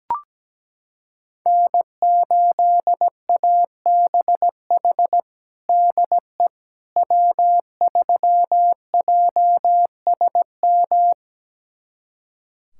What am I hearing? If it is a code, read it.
Morse code, heard as a keyed tone, about 17 wpm, 700 Hz. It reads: N8ABH DE W3JSM
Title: USA Callsign DE Callsign - 5 Characters - 17wpm